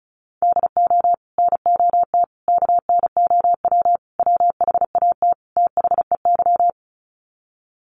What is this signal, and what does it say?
Morse code, heard as a keyed tone, about 34 wpm, 700 Hz. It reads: DO NOT KNOW WHAT THEY
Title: Sets of 5 Words 35wpm